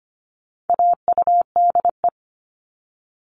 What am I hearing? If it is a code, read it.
Morse code, heard as a keyed tone, about 25 wpm, 700 Hz. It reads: AUDE
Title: All Cut Numbers - 4-Digit 25wpm